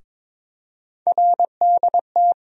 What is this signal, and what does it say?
Morse code, heard as a keyed tone, about 23 wpm, 700 Hz. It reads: RDT
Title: USA Call Sign Postfixes - 3 Letters 22wpm